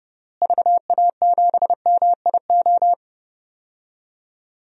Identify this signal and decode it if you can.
Morse code, heard as a keyed tone, about 30 wpm, 700 Hz. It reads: VA7MIO
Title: DXCC-100 Call Signs - 6 Characters - Repeated 3x - 30wpm